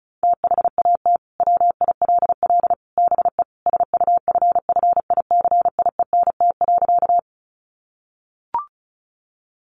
Morse code, heard as a keyed tone, about 34 wpm, 700 Hz. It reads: THAT WILL BE SUFFICIENT.
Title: Sentences from Top 1400 Words 35wpm